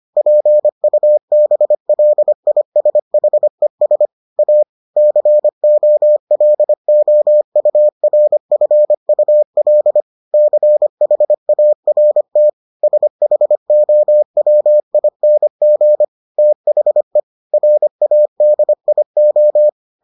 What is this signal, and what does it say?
Morse code, heard as a keyed tone, about 25 wpm, 600 Hz. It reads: PUBLISHES A COLOURFUL CHART SHOWING THE RADIO